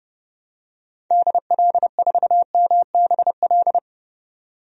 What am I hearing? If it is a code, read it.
Morse code, heard as a keyed tone, about 30 wpm, 700 Hz. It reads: DL4MBL
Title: DXCC-100 Call Signs - 6 Characters - 30wpm